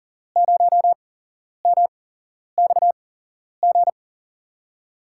Morse code, heard as a keyed tone, about 40 wpm, spaced as for 11 wpm, 700 Hz. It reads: 0MXG
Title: USA Call Signs - 5 Characters - 8x Character Spacing - 40wpm